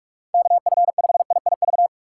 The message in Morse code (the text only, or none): KF5IIV